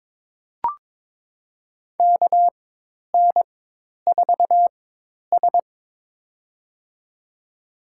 Morse code, heard as a keyed tone, about 21 wpm, spaced as for 10 wpm, 700 Hz. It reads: KN4S
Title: USA Call Signs - 4 Characters - 4x Character Spacing - 22wpm